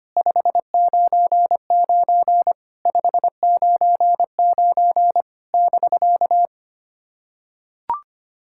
Morse code, heard as a keyed tone, about 25 wpm, 700 Hz. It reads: 599 599 <BK>
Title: SOTA QSO Practice 25wpm